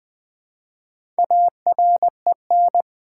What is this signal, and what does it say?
Morse code, heard as a keyed tone, about 20 wpm, 700 Hz. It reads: AREN
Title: Build A Word - 1 Letter At A Time - 5-Letter Words - 20wpm